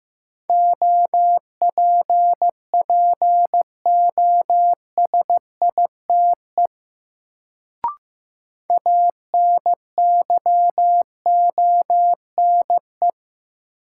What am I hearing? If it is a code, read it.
Morse code, heard as a keyed tone, about 15 wpm, 700 Hz. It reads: OPPOSITE ANYONE
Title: Sets of 5 Words Encore - New Word Drill - 15wpm